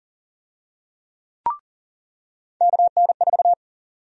Morse code, heard as a keyed tone, about 40 wpm, 700 Hz. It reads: KN4
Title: USA Call Sign Prefixes - 40wpm